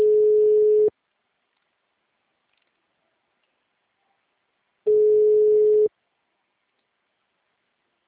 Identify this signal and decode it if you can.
Ringback tone (Europe, 425 Hz), with speech.
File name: out- -2011- - - .0